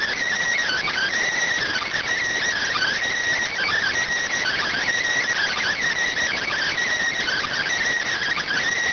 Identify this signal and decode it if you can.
Emergency Alert System (SAME) tones (header bursts), heard as harsh buzzing bursts. It is heard from 1.0 to 7.8 s.